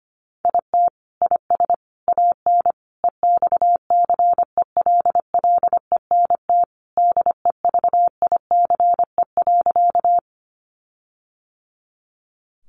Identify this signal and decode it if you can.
Morse code, heard as a keyed tone, about 25 wpm, 700 Hz. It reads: IT IS AN EXCELLENT DEVICE.
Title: Sentences from Top 1500 Words 25wpm